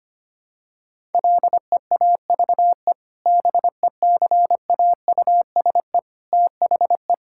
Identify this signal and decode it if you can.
Morse code, heard as a keyed tone, about 25 wpm, 700 Hz. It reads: LEAVE BECAUSE THE
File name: practice / 25wpm / Sentences with 2-Word Phrases - 25wpm